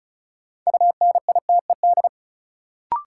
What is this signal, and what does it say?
Morse code, heard as a keyed tone, about 36 wpm, 700 Hz. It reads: UNITED
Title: Sets of 2 Words - New Word Drill - 35wpm